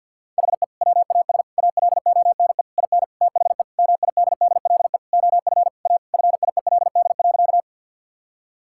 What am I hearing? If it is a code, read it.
Morse code, heard as a keyed tone, about 50 wpm, 700 Hz. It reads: HE WAS ALONE IN THE MIDDLE OF A FIELD.
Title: Sentences from Top 1000 Words - Review Part 6 - 50wpm